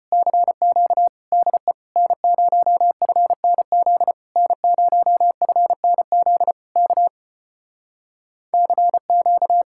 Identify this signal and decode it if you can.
Morse code, heard as a keyed tone, about 33 wpm, 700 Hz. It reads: CQ DE N0FNZ N0FNZ K CQ